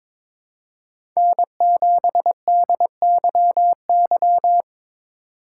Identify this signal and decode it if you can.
Morse code, heard as a keyed tone, about 21 wpm, 700 Hz. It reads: N7DYY
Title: USA Call Signs - 5 Characters - 4x Character Spacing - 22wpm